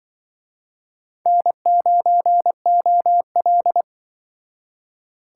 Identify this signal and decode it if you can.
Morse code, heard as a keyed tone, about 24 wpm, 700 Hz. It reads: N9OL
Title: Speed Racing - USA 1x2 Call Signs 20wpm